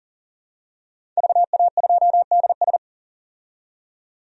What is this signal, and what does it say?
Morse code, heard as a keyed tone, about 40 wpm, 700 Hz. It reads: VA2DS
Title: DXCC-100 Call Signs - 5 Characters - 40wpm